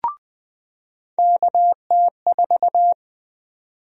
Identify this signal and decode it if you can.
Morse code, heard as a keyed tone, about 20 wpm, 700 Hz. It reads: KT4
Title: USA Call Sign Prefixes - 20wpm